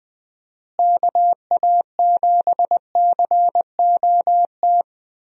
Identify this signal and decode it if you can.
Morse code, heard as a keyed tone, about 19 wpm, 700 Hz. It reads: KA7COT
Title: USA Call Signs - 6 Characters - Repeated 2x - 20wpm